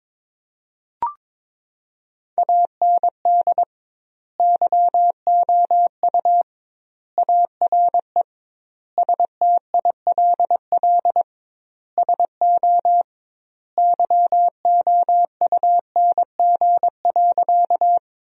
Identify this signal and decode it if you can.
Morse code, heard as a keyed tone, about 22 wpm, 700 Hz. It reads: AND YOU ARE STILL SO YOUNG.
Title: 2x Word-Spacing - Sentences from Top 400 Words - 22wpm